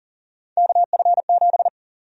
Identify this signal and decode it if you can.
Morse code, heard as a keyed tone, about 40 wpm, 700 Hz. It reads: KF7
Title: USA Call Sign Prefixes - 40wpm